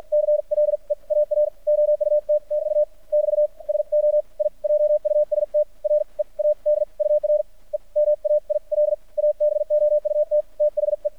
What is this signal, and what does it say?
Morse code, heard as a keyed tone, about 50 wpm, spaced as for 42 wpm, 600 Hz. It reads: CLE AA OUT TU <BT> SO I JUST READ UR EMAIL ABOUT THE
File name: archived / CRSnet